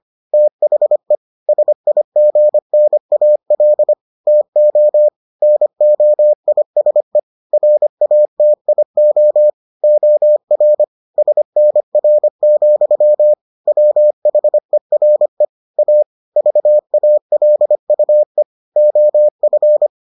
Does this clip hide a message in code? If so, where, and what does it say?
Yes: Morse code from 0.3 to 19.9 s, heard as a keyed tone, about 25 wpm, 600 Hz. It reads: THE SIGNAL TO NOISE RATIO OR SNR, WHERE A VALUE OF